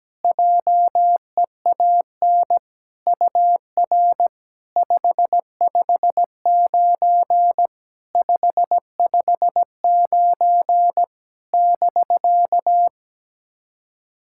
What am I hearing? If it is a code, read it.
Morse code, heard as a keyed tone, about 17 wpm, 700 Hz. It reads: JEAN UR 559 559 <BK>
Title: SOTA QSO Practice 17wpm